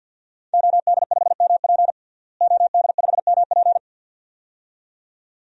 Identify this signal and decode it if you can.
Morse code, heard as a keyed tone, about 53 wpm, 700 Hz. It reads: OD5MP OD5MP
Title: DXCC-100 Call Signs - 5 Characters - Repeated 2x - 50wpm